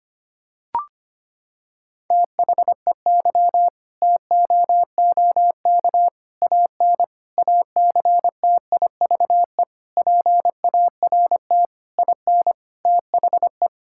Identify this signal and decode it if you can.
Morse code, heard as a keyed tone, about 25 wpm, 700 Hz. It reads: THEY TOOK AN ACTIVE PART IN THE
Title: Sentences from Top 1000 Words - Review Part 6 - 25wpm